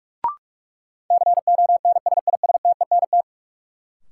Morse code, heard as a keyed tone, about 46 wpm, 700 Hz. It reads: CONSISTENT
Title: Sentences from Top 1500 Words - New Word Drill Alt - 45wpm